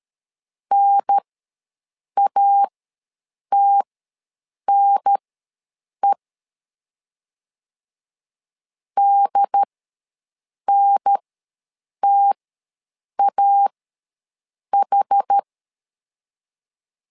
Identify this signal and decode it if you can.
Morse code, heard as a keyed tone, about 13 wpm, spaced as for 7 wpm, 800 Hz. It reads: NATNE DNTAH
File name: LEZIONI / 2 UHDN / 2 UHDN ws 12wpm 1708